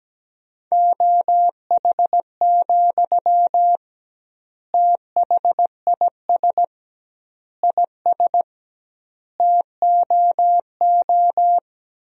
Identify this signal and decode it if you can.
Morse code, heard as a keyed tone, about 17 wpm, 700 Hz. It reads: OH, THIS IS TOO
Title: 2x Word-Spacing - Sentences from Top 500 Words - 17wpm